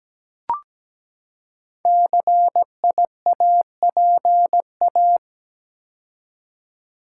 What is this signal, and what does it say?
Morse code, heard as a keyed tone, about 17 wpm, 700 Hz. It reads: CIAPA
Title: Code Groups - 5 Letters - 17wpm